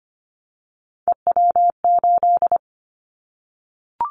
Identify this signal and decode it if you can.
Morse code, heard as a keyed tone, about 25 wpm, 700 Hz. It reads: EW8
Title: Callsign Prefixes for Top 50 DXCC - 25wpm